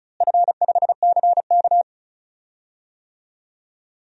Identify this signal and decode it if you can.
Morse code, heard as a keyed tone, about 36 wpm, 700 Hz. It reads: F5CK
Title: DXCC-100 Call Signs - 4 Characters - Repeated 2x - 35wpm